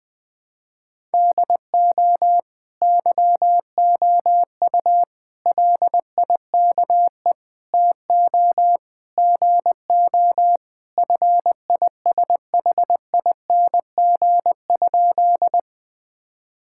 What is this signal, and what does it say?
Morse code, heard as a keyed tone, about 21 wpm, 700 Hz. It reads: DO YOU LIKE TO GO FISHING?
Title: Sentences from Top 1000 Words - Review Part 5 - 20wpm